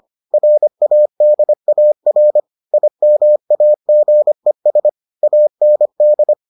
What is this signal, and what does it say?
Morse code, heard as a keyed tone, about 25 wpm, 600 Hz. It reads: RADAR IMAGES AND